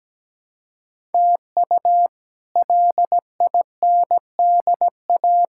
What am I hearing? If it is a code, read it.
Morse code, heard as a keyed tone, about 17 wpm, 700 Hz. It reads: TU LINDA
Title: K1USN SST QSO Practice - 17wpm